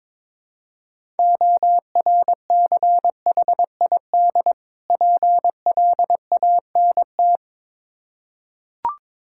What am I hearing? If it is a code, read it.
Morse code, heard as a keyed tone, about 22 wpm, 690 Hz. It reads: ORCHID PLANT
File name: Difficult Words with Hints - 22wpm